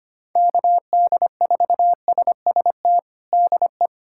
Morse code, heard as a keyed tone, about 25 wpm, 700 Hz. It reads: KD4SST DE
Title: USA Callsign DE Callsign - 6 Characters - 25wpm